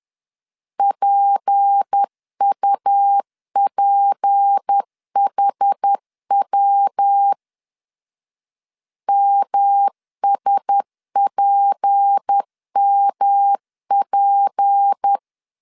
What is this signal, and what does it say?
Morse code, heard as a keyed tone, about 10 wpm, 800 Hz. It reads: PUPHW MSPMP
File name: LEZIONI / 6 PKX0 / 6 P 10wpm 2432